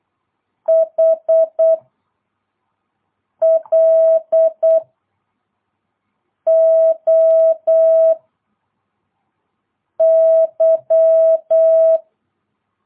Morse code, heard as a keyed tone, about 8 wpm, 660 Hz. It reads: H L O Y
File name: jh8cbh / 2022cw